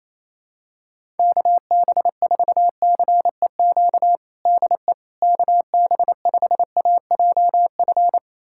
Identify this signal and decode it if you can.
Morse code, heard as a keyed tone, about 28 wpm, 700 Hz. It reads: KB4CEQ DE KB5AJF
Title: USA Callsign DE Callsign 28wpm